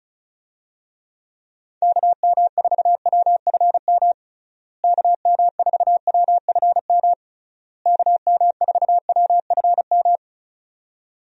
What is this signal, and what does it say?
Morse code, heard as a keyed tone, about 36 wpm, 700 Hz. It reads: KM4WFM KM4WFM KM4WFM
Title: USA Call Signs - 6 Characters - Repeated 3x - 35wpm